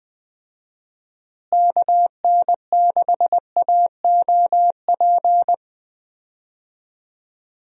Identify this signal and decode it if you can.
Morse code, heard as a keyed tone, about 20 wpm, 700 Hz. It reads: KN6AOP
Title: USA Call Signs - 6 Characters - Repeated 2x - 20wpm